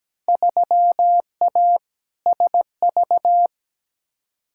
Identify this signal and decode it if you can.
Morse code, heard as a keyed tone, about 17 wpm, 700 Hz. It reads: 3A SV
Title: Field Day Exchanges - 17wpm